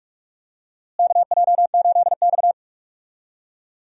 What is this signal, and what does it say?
Morse code, heard as a keyed tone, about 46 wpm, 700 Hz. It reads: KJ8X
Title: USA Call Signs - 4 Characters - Repeated 3x - 45wpm